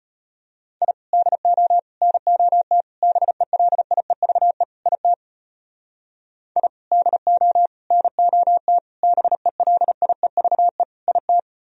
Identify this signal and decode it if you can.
Morse code, heard as a keyed tone, about 32 wpm, 700 Hz. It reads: I DO NOT BELIEVE IT I DO NOT BELIEVE IT